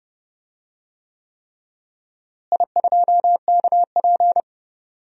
Morse code, heard as a keyed tone, about 32 wpm, 700 Hz. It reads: I2KP
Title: DXCC-100 Call Signs - 4 Characters - Repeated 3x - 30wpm